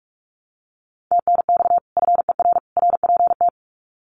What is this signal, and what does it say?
Morse code, heard as a keyed tone, about 44 wpm, 700 Hz. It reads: TNX FER RPT
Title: Common QSO Phrases 45wpm